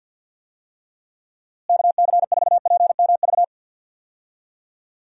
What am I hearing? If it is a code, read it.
Morse code, heard as a keyed tone, about 51 wpm, 700 Hz. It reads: KC4PMV
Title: USA Call Signs - 6 Characters - Repeated 3x - 50wpm